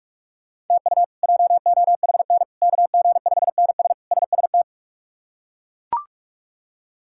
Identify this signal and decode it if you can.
Morse code, heard as a keyed tone, about 41 wpm, 700 Hz. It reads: TU JOHN KG5NS SST
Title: K1USN SST QSO Practice - 45wpm